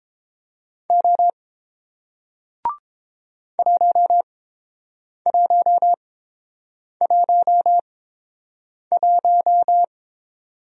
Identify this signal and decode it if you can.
Morse code, heard as a keyed tone, about 30 wpm, spaced as for 7 wpm, 700 Hz. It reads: O 1111